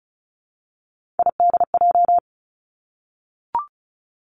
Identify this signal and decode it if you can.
Morse code, heard as a keyed tone, about 36 wpm, 700 Hz. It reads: IDJ
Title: Three Challenging Letter-Number Combinations 35wpm